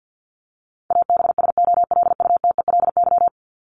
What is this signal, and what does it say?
Morse code, heard as a keyed tone, about 50 wpm, 700 Hz. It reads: ABSOLUTELY
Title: LY Suffix - Top 100 Words with Base Words 50wpm